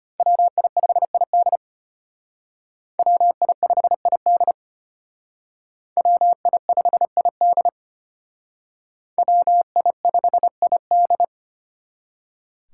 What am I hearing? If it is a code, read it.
Morse code, heard as a keyed tone, about 34 wpm, 700 Hz. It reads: WI5ID WI5ID WI5ID WI5ID